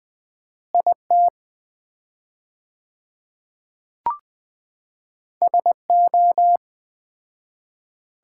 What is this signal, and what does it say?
Morse code, heard as a keyed tone, about 21 wpm, 700 Hz. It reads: IT SO
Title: 4x Word-Spacing - Sets of 2 Words - 20wpm